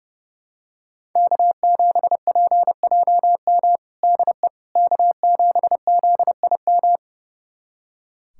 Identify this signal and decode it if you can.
Morse code, heard as a keyed tone, about 30 wpm, 700 Hz. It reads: K7PJM DE K7ZIM
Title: USA Callsign DE Callsign - 5 Characters - 30wpm